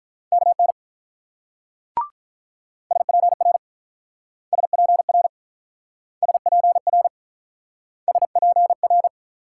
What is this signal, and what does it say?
Morse code, heard as a keyed tone, about 47 wpm, 700 Hz. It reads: KN SPR SPR SPR SPR